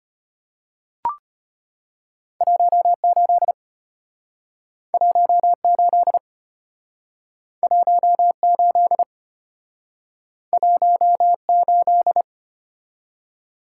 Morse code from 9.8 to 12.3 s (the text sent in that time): 18